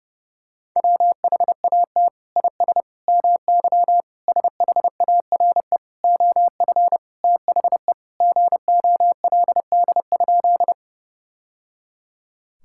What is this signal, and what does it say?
Morse code, heard as a keyed tone, about 30 wpm, 700 Hz. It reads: WHAT IS MY SHARE OF THE GOLD?
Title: Sentences from Top 1000 Words - Review Part 6 - 30wpm